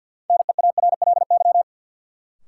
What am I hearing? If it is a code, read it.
Morse code, heard as a keyed tone, about 51 wpm, 700 Hz. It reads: NEARLY
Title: Sentences from Top 1300 Words - New Word Drill Alt - 50wpm